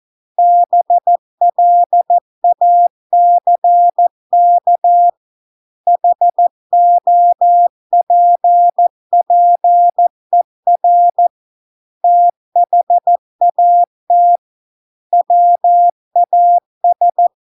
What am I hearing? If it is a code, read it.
Morse code, heard as a keyed tone, about 14 wpm, 700 Hz. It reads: BLACK HOPPER THAT WAS